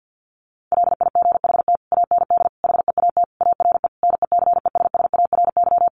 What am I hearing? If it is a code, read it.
Morse code, heard as a keyed tone, about 50 wpm, 700 Hz. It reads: LIGHT AND HEAT ARE NECESSARY
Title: Sentences from Top 1300 Words 50wpm